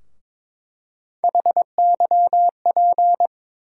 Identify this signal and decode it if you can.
Morse code, heard as a keyed tone, about 22 wpm, 700 Hz. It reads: HYP
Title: USA Call Sign Postfixes - 3 Letters 22wpm